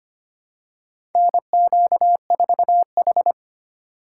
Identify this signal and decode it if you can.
Morse code, heard as a keyed tone, about 25 wpm, 700 Hz. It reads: NQ4H
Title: USA Call Signs - 3x Character Spacing - 25wpm